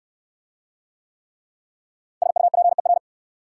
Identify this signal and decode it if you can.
Morse code, heard as a keyed tone, about 70 wpm, 690 Hz. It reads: SUPER